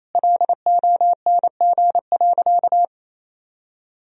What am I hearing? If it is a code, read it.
Morse code, heard as a keyed tone, about 28 wpm, 700 Hz. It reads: LONG.
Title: Sentences from Top 1000 Words - Review Part 1 - 28wpm